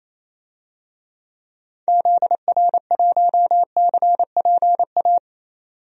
Morse code, heard as a keyed tone, about 28 wpm, 700 Hz. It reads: ZR1CPA
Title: DXCC-100 Call Signs - 6 Characters - 28wpm